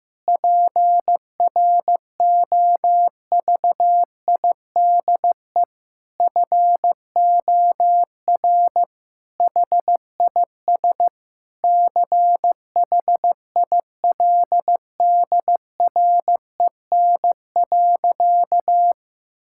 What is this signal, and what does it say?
Morse code, heard as a keyed tone, about 15 wpm, 700 Hz. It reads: PROVIDE FOR HIS CHILDREN.
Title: Sentences from Top 1600 Words 15wpm